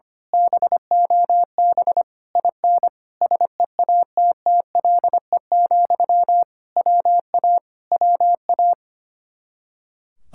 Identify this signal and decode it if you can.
Morse code, heard as a keyed tone, about 24 wpm, 700 Hz. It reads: BOB IN SEATTLE, WA WA
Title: Speed-Racing Common QSO Phrases - 25wpm